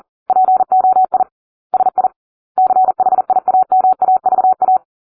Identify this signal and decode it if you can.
Morse code, heard as a keyed tone, about 40 wpm, 750 Hz. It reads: POI SI CHIAMAVA